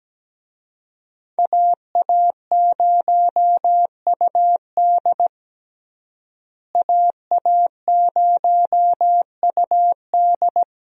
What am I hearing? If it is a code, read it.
Morse code, heard as a keyed tone, about 17 wpm, 700 Hz. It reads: AA0UD AA0UD
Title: USA Call Signs - 5 Characters - Repeated 3x - 17wpm